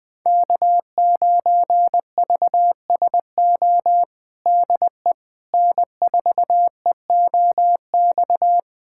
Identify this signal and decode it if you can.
Morse code, heard as a keyed tone, about 21 wpm, 700 Hz. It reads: K9VSO DE N4EOX
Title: USA Callsign DE Callsign - 5 Characters - 20wpm